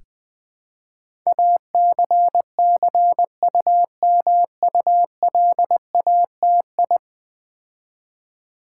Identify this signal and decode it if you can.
Morse code, heard as a keyed tone, about 20 wpm, 700 Hz. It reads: ACCUMULATI
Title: Build A Word - 5 Letters At A Time - 12-Letter Words - 20wpm